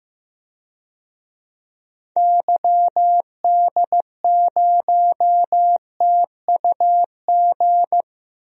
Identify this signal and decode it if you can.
Morse code, heard as a keyed tone, about 15 wpm, 700 Hz. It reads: YD0TUG
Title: DXCC-100 Call Signs - 6 Characters - Repeated 2x - 15wpm